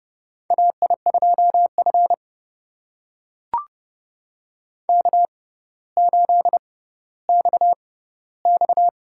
Morse code, heard as a keyed tone, about 30 wpm, spaced as for 15 wpm, 700 Hz. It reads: AI2F K8XX
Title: USA Call Signs - 4 Characters - 6x Character Spacing - 30wpm